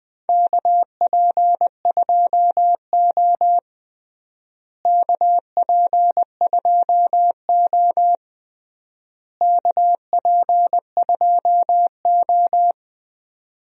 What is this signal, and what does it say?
Morse code, heard as a keyed tone, about 20 wpm, 700 Hz. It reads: KP2O KP2O KP2O
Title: DXCC-100 Call Signs - 4 Characters - Repeated 3x - 20wpm